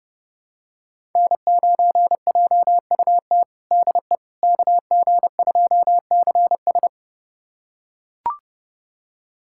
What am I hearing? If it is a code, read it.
Morse code, heard as a keyed tone, about 32 wpm, 700 Hz. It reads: N9JUT DE KG2CS
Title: USA Callsign DE Callsign - 5 Characters - 30wpm